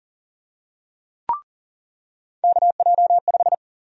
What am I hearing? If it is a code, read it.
Morse code, heard as a keyed tone, about 40 wpm, 700 Hz. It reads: KJ5
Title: USA Call Sign Prefixes - 40wpm